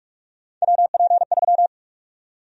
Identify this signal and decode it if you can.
Morse code, heard as a keyed tone, about 46 wpm, 700 Hz. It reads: WP3
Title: USA Call Sign Prefixes - 45wpm